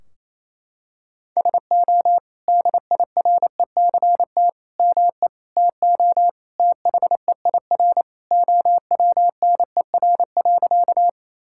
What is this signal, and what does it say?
Morse code, heard as a keyed tone, about 28 wpm, 700 Hz. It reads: SO DIRECT ME TO THEIR OWNER.
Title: Sentences from Top 1600 Words 28wpm